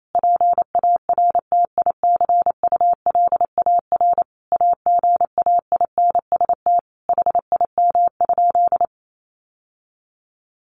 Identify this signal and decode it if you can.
Morse code, heard as a keyed tone, about 28 wpm, 700 Hz. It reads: PARTICULAR AGAINST HIM?
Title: Sentences from Top 1000 Words 28wpm